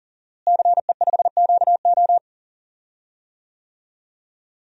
Morse code, heard as a keyed tone, about 40 wpm, 700 Hz. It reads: CE5QO
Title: DXCC-100 Call Signs - 5 Characters - Repeated 2x - 40wpm